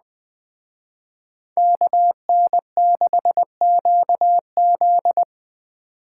Morse code, heard as a keyed tone, about 20 wpm, 700 Hz. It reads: KN6QZ
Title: Speed Racing - USA 2x2 Call Signs 17wpm